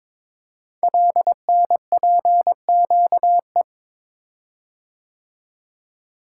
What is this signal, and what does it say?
Morse code, heard as a keyed tone, about 23 wpm, 700 Hz. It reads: LNPQE
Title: Code Groups - 5 Letters - 22wpm